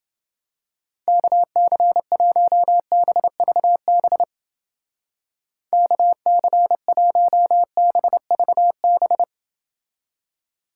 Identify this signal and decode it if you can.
Morse code, heard as a keyed tone, about 29 wpm, 700 Hz. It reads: KC1BVB KC1BVB